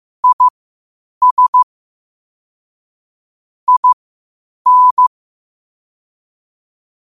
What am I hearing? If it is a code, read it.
Morse code, heard as a keyed tone, about 15 wpm, spaced as for 8 wpm, 1000 Hz. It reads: IS IN
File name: qotdfiles / 8wpm